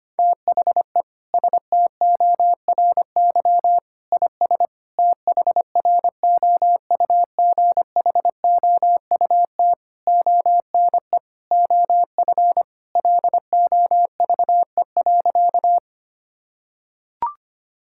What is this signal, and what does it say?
Morse code, heard as a keyed tone, about 25 wpm, 700 Hz. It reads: THE STORY IS THROUGHOUT ONE OF LOVE.
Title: Sentences from Top 1000 Words - Review Part 5 - 25wpm